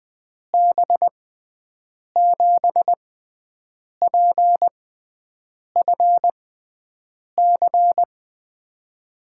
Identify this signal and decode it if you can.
Morse code, heard as a keyed tone, about 20 wpm, spaced as for 7 wpm, 700 Hz. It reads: B7PFC
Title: USA Call Signs - 6 Characters - 6x Character Spacing - 20wpm